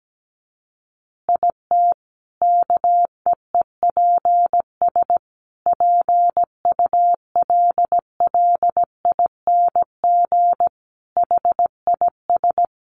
Morse code, heard as a keyed tone, about 17 wpm, 700 Hz. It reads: IT KEEPS PULLING HIS
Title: Additional Easy Phrases 17wpm